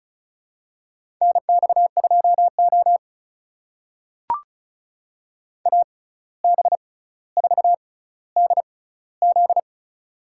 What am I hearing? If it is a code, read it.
Morse code, heard as a keyed tone, about 36 wpm, spaced as for 17 wpm, 700 Hz. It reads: NX2O AB4DZ
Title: USA Call Signs - 6x Character Spacing - 35wpm